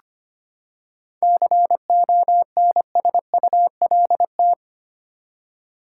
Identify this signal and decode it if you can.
Morse code, heard as a keyed tone, about 25 wpm, 700 Hz. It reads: CONSULT
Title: Seven Letter Words - 25wpm